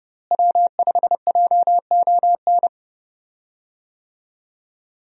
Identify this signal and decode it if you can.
Morse code, heard as a keyed tone, about 31 wpm, 700 Hz. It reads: W5JON
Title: USA Call Signs - 5 Characters - Repeated 2x - 30wpm